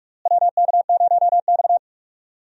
Morse code, heard as a keyed tone, about 46 wpm, 700 Hz. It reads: WK0X
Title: USA Call Signs - 4 Characters - Repeated 2x - 45wpm